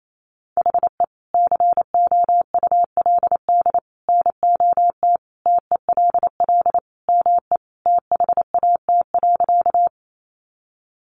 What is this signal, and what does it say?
Morse code, heard as a keyed tone, about 28 wpm, 700 Hz. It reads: HE COULD NOT TELL ME THAT.
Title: Sentences from Top 600 Words 28wpm